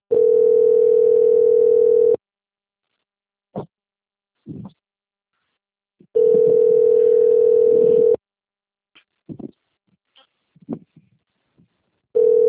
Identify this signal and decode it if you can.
Ringback tone (North America), 440 + 480 Hz.
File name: out- - - - - .2220